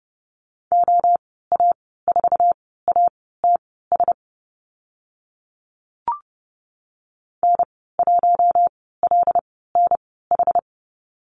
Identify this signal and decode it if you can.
Morse code, heard as a keyed tone, about 30 wpm, spaced as for 17 wpm, 700 Hz. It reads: OA4ATS N1LNH
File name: Lesson 14g - Number 2 - Call Signs 30wpm